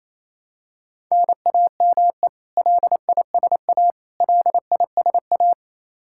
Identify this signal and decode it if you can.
Morse code, heard as a keyed tone, about 28 wpm, 700 Hz. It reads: NAME LISA LISA
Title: Practice QSO - Exchange 1 - Beginning - 28wpm